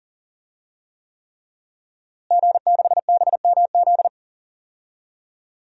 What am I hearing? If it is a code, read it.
Morse code, heard as a keyed tone, about 40 wpm, 700 Hz. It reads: G6BMZ
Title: DXCC-100 Call Signs - 5 Characters - Repeated 3x - 40wpm